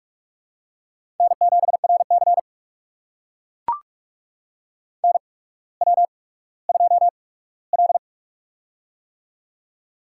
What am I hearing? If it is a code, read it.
Morse code, heard as a keyed tone, about 44 wpm, spaced as for 19 wpm, 700 Hz. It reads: N7RC NW2L
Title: USA Call Signs - 4 Characters - 8x Character Spacing - 45wpm